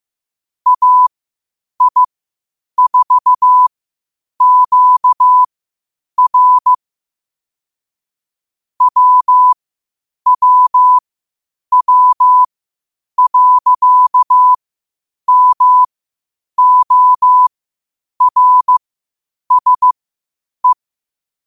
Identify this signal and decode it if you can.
Morse code, heard as a keyed tone, about 15 wpm, spaced as for 8 wpm, 1000 Hz. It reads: AI4QR WWW.MORSE
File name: qotdfiles / 8wpm